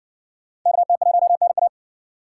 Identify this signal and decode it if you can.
Morse code, heard as a keyed tone, about 57 wpm, 700 Hz. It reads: KT1NA